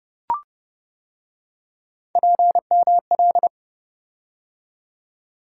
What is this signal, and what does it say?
Morse code, heard as a keyed tone, about 28 wpm, 700 Hz. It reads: PML